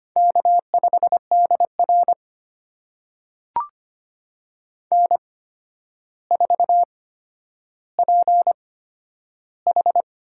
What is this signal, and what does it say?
Morse code, heard as a keyed tone, about 25 wpm, spaced as for 11 wpm, 700 Hz. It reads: K5DR N4PH
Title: USA Call Signs - 4 Characters - 8x Character Spacing - 25wpm